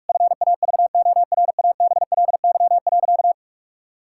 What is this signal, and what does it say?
Morse code, heard as a keyed tone, about 41 wpm, 700 Hz. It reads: FAVORABLY.
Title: Pangram Challenge v2 - 45wpm